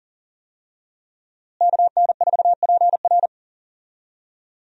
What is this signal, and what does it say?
Morse code, heard as a keyed tone, about 40 wpm, 700 Hz. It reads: KN4PR
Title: USA Call Signs - 5 Characters - Repeated 2x - 40wpm